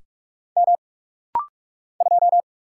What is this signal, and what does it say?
Morse code, heard as a keyed tone, about 45 wpm, 700 Hz. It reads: M 2
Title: Single Letter-Number - Rapid-Fire 45wpm